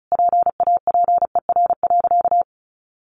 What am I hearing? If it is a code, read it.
Morse code, heard as a keyed tone, about 34 wpm, 700 Hz. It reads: PAPER.
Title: Sentences from Top 1200 Words 35wpm